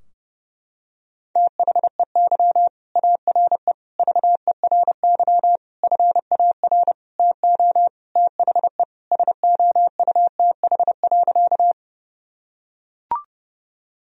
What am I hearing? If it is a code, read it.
Morse code, heard as a keyed tone, about 30 wpm, 700 Hz. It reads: THEY ARE VERY FAR TO THE SOUTH.
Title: Sentences from Top 1000 Words - Review Part 2 - 30wpm